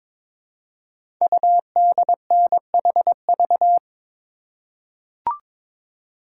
Morse code, heard as a keyed tone, about 22 wpm, 700 Hz. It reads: UDNHV
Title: Code Groups - 5 Letters - 22wpm